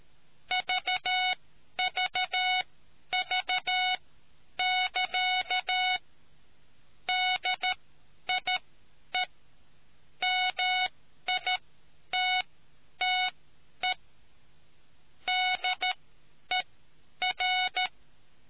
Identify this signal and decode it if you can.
Morse code, heard as a keyed tone, about 13 wpm, spaced as for 9 wpm, 740 Hz. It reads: VVV<KA> DIE MITTE DER